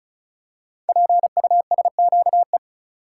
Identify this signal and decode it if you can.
Morse code, heard as a keyed tone, about 36 wpm, 690 Hz. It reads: PUSQE
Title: Code Groups - 5 Letters - 35wpm